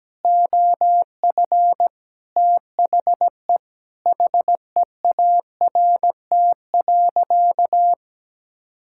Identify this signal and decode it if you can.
Morse code, heard as a keyed tone, about 17 wpm, 700 Hz. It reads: OF THE HEART.
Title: Sentences from Top 1600 Words 17wpm